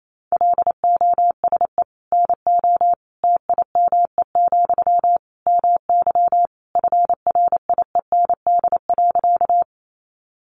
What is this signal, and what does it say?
Morse code, heard as a keyed tone, about 28 wpm, 700 Hz. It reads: LOSE NO TIME, MY FRIEND.
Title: Sentences from Top 1100 Words 28wpm